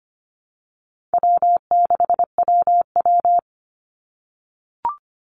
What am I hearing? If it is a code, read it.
Morse code, heard as a keyed tone, about 25 wpm, 700 Hz. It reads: W6WW
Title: NA SOTA Chaser Callsigns 25wpm